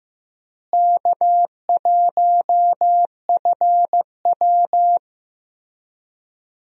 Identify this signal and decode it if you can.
Morse code, heard as a keyed tone, about 15 wpm, 700 Hz. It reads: K1FW
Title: USA Call Signs - 4 Characters - Repeated 3x - 15wpm